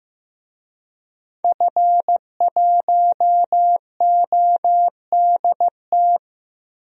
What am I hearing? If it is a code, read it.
Morse code, heard as a keyed tone, about 15 wpm, 700 Hz. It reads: F1ODT
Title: DXCC-100 Call Signs - 5 Characters - 15wpm